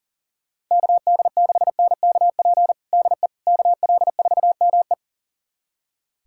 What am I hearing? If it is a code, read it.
Morse code, heard as a keyed tone, about 40 wpm, 700 Hz. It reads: KD6NKP DE KL4ME
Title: USA Callsign DE Callsign 40wpm